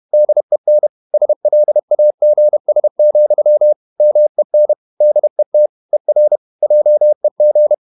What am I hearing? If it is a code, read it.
Morse code, heard as a keyed tone, about 31 wpm, 600 Hz. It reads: DEN SLAGS, MEN DET ER JEG
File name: Ferie QRQ 31wpm